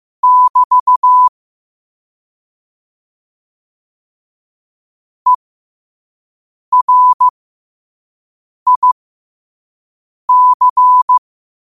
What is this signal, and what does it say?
Morse code, heard as a keyed tone, about 15 wpm, spaced as for 5 wpm, 1000 Hz. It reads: <BT> ERIC
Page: New quotes every day in morse code at 5 Words per minute.